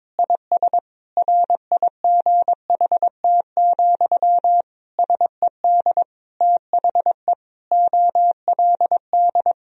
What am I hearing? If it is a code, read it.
Morse code, heard as a keyed tone, about 22 wpm, 700 Hz. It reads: IS RIGHT, SED THE OLD
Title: Sentences from Top 200 Words and Abbreviations 22wpm